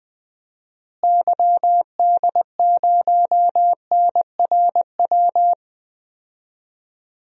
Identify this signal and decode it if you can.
Morse code, heard as a keyed tone, about 20 wpm, 700 Hz. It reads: YD0NRW
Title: DXCC-100 Call Signs - 6 Characters - Repeated 3x - 20wpm